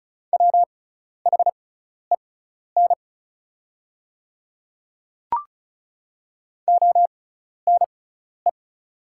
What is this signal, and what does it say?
Morse code, heard as a keyed tone, about 34 wpm, spaced as for 12 wpm, 700 Hz. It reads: WHEN ONE
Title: 6x Character Spacing - Top 200 Words - 35wpm